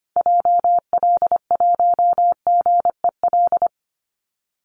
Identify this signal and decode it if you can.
Morse code, heard as a keyed tone, about 25 wpm, 700 Hz. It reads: JL1GEL
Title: Call Signs Repeated 3x - 25wpm